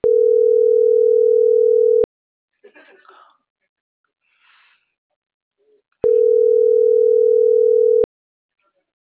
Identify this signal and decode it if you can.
Ringback tone (North America), 440 + 480 Hz.
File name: CENTRAL MABE / out edwinomarojeda - .80524